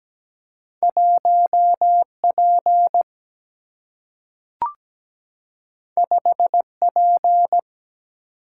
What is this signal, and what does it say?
Morse code, heard as a keyed tone, about 17 wpm, 700 Hz. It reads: 1 P 5 P
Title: Single-Letter Single-Number 17wpm